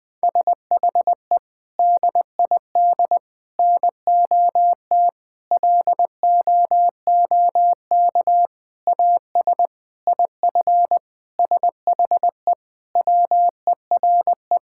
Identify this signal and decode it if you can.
Morse code, heard as a keyed tone, about 20 wpm, 700 Hz. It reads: SHE DID NOT LOOK AS IF SHE WERE
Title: Sentences from Top 1000 Words - Review Part 6 - 20wpm